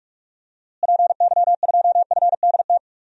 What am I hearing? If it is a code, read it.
Morse code, heard as a keyed tone, about 47 wpm, 700 Hz. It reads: PY2FDT
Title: DXCC-100 Call Signs - 6 Characters - Repeated 2x - 45wpm